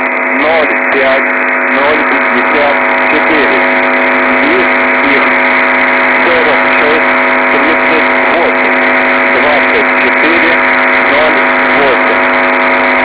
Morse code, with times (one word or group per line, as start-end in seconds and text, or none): none